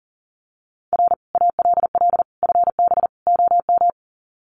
Morse code, heard as a keyed tone, about 40 wpm, 700 Hz. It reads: R ALL FB OM
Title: Common QSO Phrases 40wpm